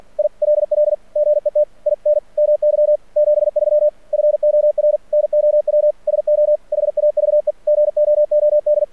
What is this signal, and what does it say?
Morse code, heard as a keyed tone, about 46 wpm, 600 Hz. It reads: ILL GET IN MY 73 FOR NOW SO HAVE GOOD
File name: archived / CRSnet